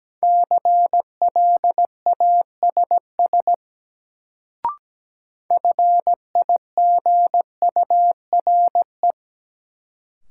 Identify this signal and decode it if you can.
Morse code, heard as a keyed tone, about 17 wpm, 700 Hz. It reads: CLASS FIGURE
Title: Sentences from Top 1000 Words - Review Part 2 - New Word Drill - 17wpm